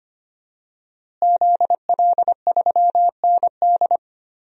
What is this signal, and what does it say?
Morse code, heard as a keyed tone, about 25 wpm, 700 Hz. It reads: ZL3ND
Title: DXCC-100 Call Signs - 5 Characters - Repeated 3x - 25wpm